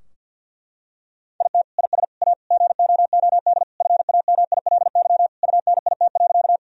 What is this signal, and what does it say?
Morse code, heard as a keyed tone, about 51 wpm, 700 Hz. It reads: IT IS A GOOD FAMILY UNIT.
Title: Sentences from Top 1000 Words - Review Part 2 - 50wpm